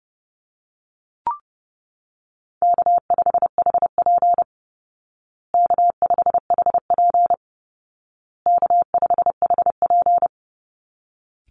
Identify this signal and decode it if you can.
Morse code, heard as a keyed tone, about 30 wpm, 700 Hz. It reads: K5HP K5HP K5HP
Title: Easy Call Signs Repeated 3x - 30wpm